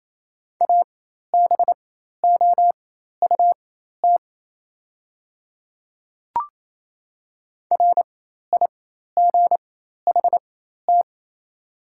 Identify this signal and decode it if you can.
Morse code, heard as a keyed tone, about 27 wpm, spaced as for 13 wpm, 690 Hz. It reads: ABOUT RIGHT
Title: 4x Character Spacing - Top 500 Words - 28wpm